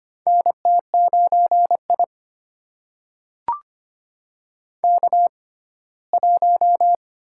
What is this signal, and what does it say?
Morse code, heard as a keyed tone, about 25 wpm, spaced as for 17 wpm, 700 Hz. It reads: NT9I K1
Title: USA Call Signs - 4 Characters - 6x Character Spacing - 25wpm